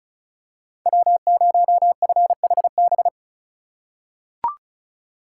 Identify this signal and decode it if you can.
Morse code, heard as a keyed tone, about 36 wpm, 700 Hz. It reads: W0FHB
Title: USA Call Signs - 5 Characters - 4x Character Spacing - 35wpm